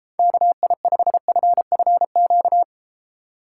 Morse code, heard as a keyed tone, about 32 wpm, 700 Hz. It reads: KI5FFQ